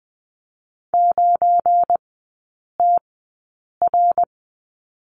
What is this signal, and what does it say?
Morse code, heard as a keyed tone, about 20 wpm, spaced as for 8 wpm, 700 Hz. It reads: 9TR
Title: 178 Lesson 29e - Slash - Three Characters 20wpm